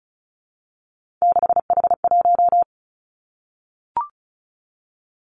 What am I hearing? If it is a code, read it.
Morse code, heard as a keyed tone, about 36 wpm, 700 Hz. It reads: 6H1
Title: Three Challenging Letter-Number Combinations 35wpm